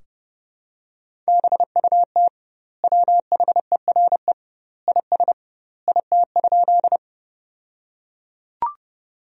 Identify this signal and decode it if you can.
Morse code, heard as a keyed tone, about 30 wpm, 700 Hz. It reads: BUT WHERE IS IT?
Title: 2x Word-Spacing - Sentences from Top 200 Words - 30wpm